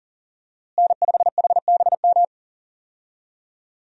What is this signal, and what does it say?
Morse code, heard as a keyed tone, about 39 wpm, 700 Hz. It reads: N5HBM
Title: USA Call Signs - 5 Characters - Repeated 3x - 40wpm